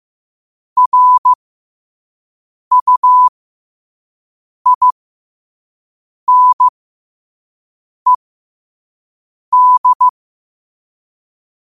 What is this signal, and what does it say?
Morse code, heard as a keyed tone, about 15 wpm, spaced as for 5 wpm, 1000 Hz. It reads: RUINED
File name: qotdfiles / 5wpm